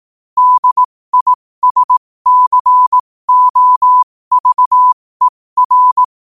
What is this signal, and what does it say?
Morse code, heard as a keyed tone, about 18 wpm, spaced as for 15 wpm, 1000 Hz. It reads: DISCOVER
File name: qotdfiles / 15wpm